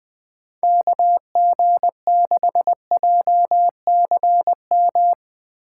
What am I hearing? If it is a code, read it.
Morse code, heard as a keyed tone, about 19 wpm, 700 Hz. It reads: KG6JCM
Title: Speed Racing - USA 2x3 Call Signs 17wpm